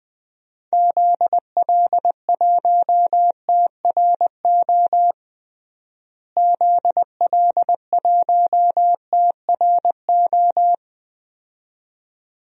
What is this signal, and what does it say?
Morse code, heard as a keyed tone, about 21 wpm, 700 Hz. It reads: ZL1TRO ZL1TRO
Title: DXCC-100 Call Signs - 6 Characters - Repeated 2x - 20wpm